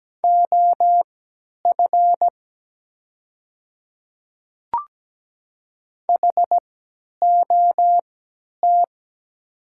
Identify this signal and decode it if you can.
Morse code, heard as a keyed tone, about 17 wpm, spaced as for 10 wpm, 700 Hz. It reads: OF HOT
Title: 3x Character Spacing - Top 200 Words - 17wpm